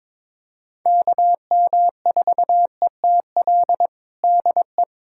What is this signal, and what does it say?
Morse code, heard as a keyed tone, about 22 wpm, 700 Hz. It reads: KM4ETL DE
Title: USA Callsign DE Callsign - 6 Characters - 22wpm